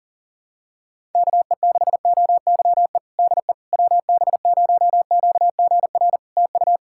Morse code, heard as a keyed tone, about 38 wpm, 700 Hz. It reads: KE6OYE DE WB0QGR TU
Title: Practice QSO - Exchange 4 - Ending - 40wpm